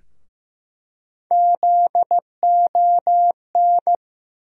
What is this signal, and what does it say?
Morse code, heard as a keyed tone, about 15 wpm, 700 Hz. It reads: ZON
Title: USA Call Sign Postfixes - 3 Letters 15wpm